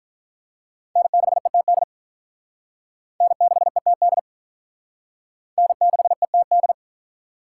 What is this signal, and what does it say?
Morse code, heard as a keyed tone, about 46 wpm, 700 Hz. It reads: N6ETD N6ETD N6ETD